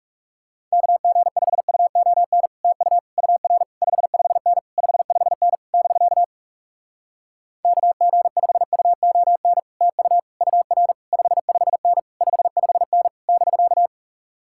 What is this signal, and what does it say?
Morse code, heard as a keyed tone, about 40 wpm, 700 Hz. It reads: KG5UON TU UR 55N 55N <BK> KG5UON TU UR 55N 55N <BK>